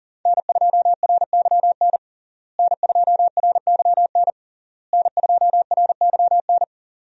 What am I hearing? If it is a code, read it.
Morse code, heard as a keyed tone, about 40 wpm, 700 Hz. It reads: N2RYN N2RYN N2RYN
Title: USA Call Signs - 5 Characters - Repeated 3x - 40wpm